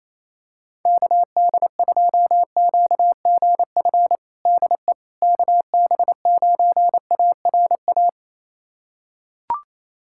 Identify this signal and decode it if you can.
Morse code, heard as a keyed tone, about 28 wpm, 700 Hz. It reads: KD2QGF DE KB9ARA
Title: USA Callsign DE Callsign - 6 Characters - 28wpm